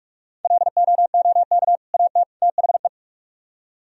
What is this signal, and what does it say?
Morse code, heard as a keyed tone, about 67 wpm, spaced as for 55 wpm, 700 Hz. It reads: LOOK AT THE
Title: Speed-Racing - Sets of 3 Words 45wpm